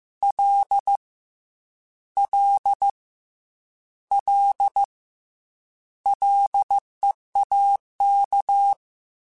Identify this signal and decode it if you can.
Morse code, heard as a keyed tone, about 15 wpm, 800 Hz. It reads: L L L LEAK
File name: Combinations / Combinations-mp3s / Combinations@13